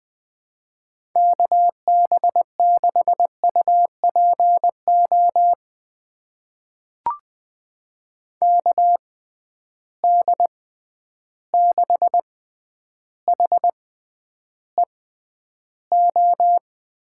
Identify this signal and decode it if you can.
Morse code, heard as a keyed tone, about 20 wpm, spaced as for 10 wpm, 700 Hz. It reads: KB6UPO KD6HEO
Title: USA Call Signs - 6 Characters - 6x Character Spacing - 20wpm